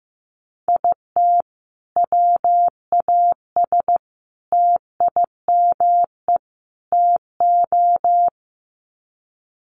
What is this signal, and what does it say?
Morse code, heard as a keyed tone, about 15 wpm, 700 Hz. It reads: IT WAS TIME TO
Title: Sets of 4 Words 15wpm